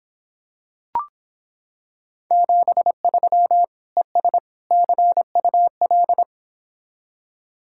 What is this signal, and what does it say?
Morse code, heard as a keyed tone, about 26 wpm, 700 Hz. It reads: 73 ES CUL